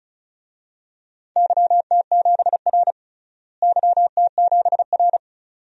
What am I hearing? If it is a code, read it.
Morse code, heard as a keyed tone, about 36 wpm, 700 Hz. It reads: YT7R YT7R
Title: DXCC-100 Call Signs - 4 Characters - Repeated 2x - 35wpm